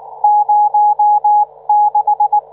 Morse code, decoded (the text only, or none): 06